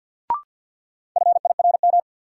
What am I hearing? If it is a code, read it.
Morse code, heard as a keyed tone, about 52 wpm, 700 Hz. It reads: FIRM
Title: Sentences from Top 1400 Words - New Word Drill - 50wpm